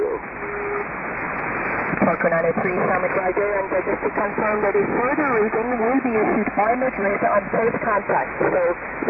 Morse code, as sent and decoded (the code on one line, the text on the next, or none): none